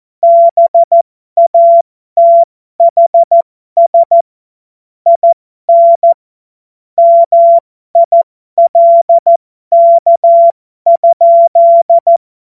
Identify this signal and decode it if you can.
Morse code, heard as a keyed tone, about 14 wpm, spaced as for 12 wpm, 680 Hz. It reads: BATHS IN MILK?
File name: Morsekurs / Witze